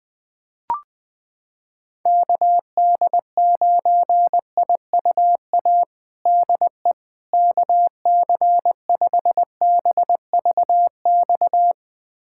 Morse code, heard as a keyed tone, about 20 wpm, 700 Hz. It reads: KD9IUA DE KC5BVX
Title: USA Callsign DE Callsign 20wpm